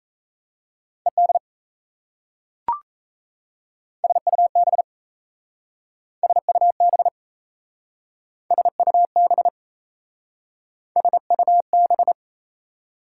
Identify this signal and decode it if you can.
Morse code, heard as a keyed tone, about 36 wpm, 700 Hz. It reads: ED SUB SUB SUB SUB